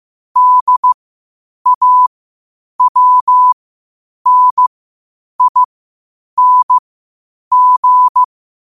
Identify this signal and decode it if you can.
Morse code, heard as a keyed tone, about 15 wpm, spaced as for 8 wpm, 1000 Hz. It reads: DAWNING
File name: qotdfiles / 8wpm